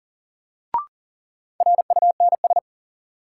Morse code, heard as a keyed tone, about 40 wpm, 700 Hz. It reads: RUNS
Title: Sentences from Top 600 Words - New Word Drill - 40wpm